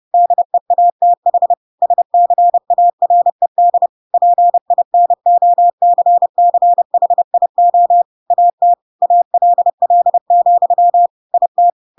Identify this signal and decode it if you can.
Morse code, heard as a keyed tone, about 30 wpm, 700 Hz. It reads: DEATH SCARED PINOCCHIO AT ALL, IT